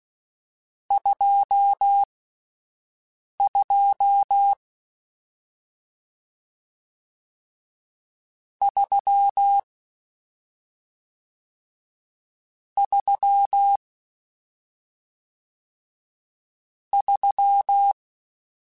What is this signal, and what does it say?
Morse code, heard as a keyed tone, about 16 wpm, spaced as for 5 wpm, 800 Hz. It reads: 22 3 3 3